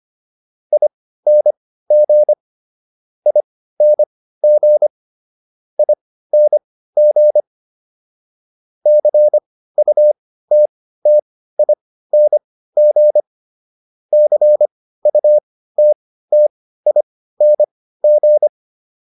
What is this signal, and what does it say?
Morse code, heard as a keyed tone, about 25 wpm, spaced as for 15 wpm, 600 Hz. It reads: ING ING ING CUTTING CUTTING